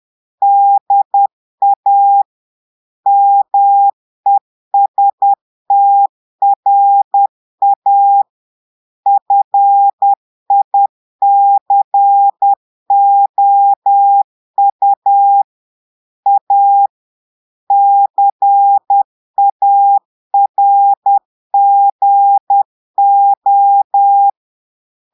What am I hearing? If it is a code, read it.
Morse code, heard as a keyed tone, about 10 wpm, 800 Hz. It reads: DA MESTRA FICOU A CARGO
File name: Portugais - texte en clair 1 - vitesse - 10 mots minute